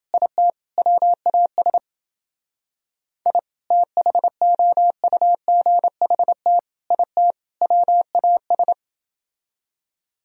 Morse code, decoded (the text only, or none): IT WAS I THOUGHT IT WAS